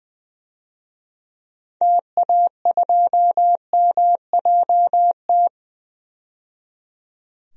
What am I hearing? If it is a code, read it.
Morse code, heard as a keyed tone, about 20 wpm, 700 Hz. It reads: TA2MJT
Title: DXCC-100 Call Signs - 6 Characters - 20wpm